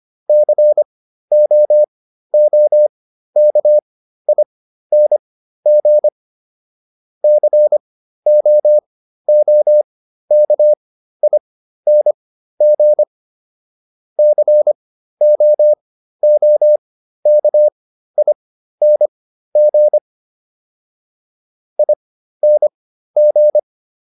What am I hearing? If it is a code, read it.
Morse code, heard as a keyed tone, about 25 wpm, spaced as for 13 wpm, 600 Hz. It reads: COOKING COOKING COOKING ING